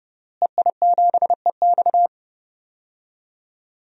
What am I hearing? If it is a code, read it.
Morse code, heard as a keyed tone, about 30 wpm, 700 Hz. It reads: EI7EX
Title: DXCC-100 Call Signs - 5 Characters - Repeated 3x - 30wpm